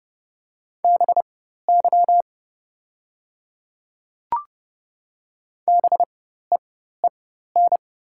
Morse code, heard as a keyed tone, about 30 wpm, spaced as for 14 wpm, 690 Hz. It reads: BY BEEN
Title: 4x Character Spacing - Top 300 Words - 30wpm